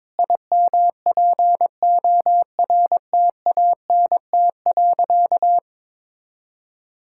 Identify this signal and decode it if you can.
Morse code, heard as a keyed tone, about 22 wpm, 700 Hz. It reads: IMPORTANT.
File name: practice / 22wpm / Sentences with 4-Word Phrases - 22wpm